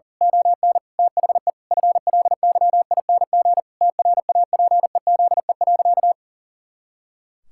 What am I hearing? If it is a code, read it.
Morse code, heard as a keyed tone, about 40 wpm, 700 Hz. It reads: ON THE FLYING TRAPEZE.
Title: Pangram Challenge v2 - 40wpm